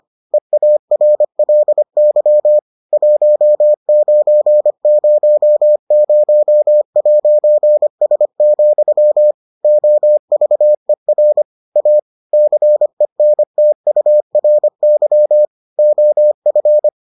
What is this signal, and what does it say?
Morse code, heard as a keyed tone, about 25 wpm, 600 Hz. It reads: EARLY 1900'S, OVER A CENTURY OF